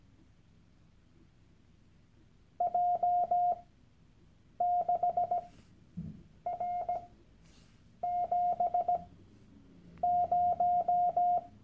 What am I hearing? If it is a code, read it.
Morse code, heard as a keyed tone, about 17 wpm, spaced as for 7 wpm, 710 Hz. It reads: J6R70
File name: Workshop Dateien / CW-Online-Kurs / Lektion 001